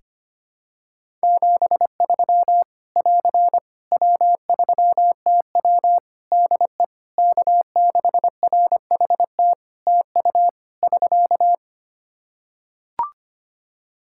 Morse code, heard as a keyed tone, about 25 wpm, 700 Hz. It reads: 73 <AR> W3TW DE K6RHT TU <SK>
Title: Practice QSO - Exchange 4 - Ending - 25wpm